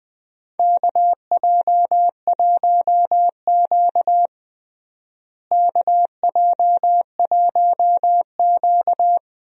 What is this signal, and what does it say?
Morse code, heard as a keyed tone, about 20 wpm, 700 Hz. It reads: KJ1Q KJ1Q
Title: USA Call Signs - 4 Characters - Repeated 2x - 20wpm